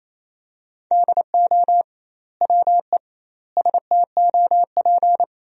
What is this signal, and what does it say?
Morse code, heard as a keyed tone, about 28 wpm, 700 Hz. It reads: DO WE STOP
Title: 2x Word-Spacing - Sentences from Top 300 Words - 28wpm